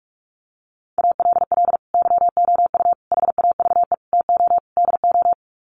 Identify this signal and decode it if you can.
Morse code, heard as a keyed tone, about 44 wpm, 700 Hz. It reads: ALL YOU HAVE TO DO
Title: Sets of 5 Words 45wpm